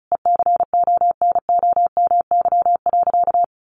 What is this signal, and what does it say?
Morse code, heard as a keyed tone, about 34 wpm, 700 Hz. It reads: ECONOMY.
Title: Sentences from Top 1200 Words 35wpm